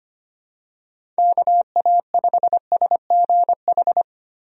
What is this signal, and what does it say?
Morse code, heard as a keyed tone, about 25 wpm, 700 Hz. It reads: KA5SGH
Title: USA Call Signs - 6 Characters - Repeated 2x - 25wpm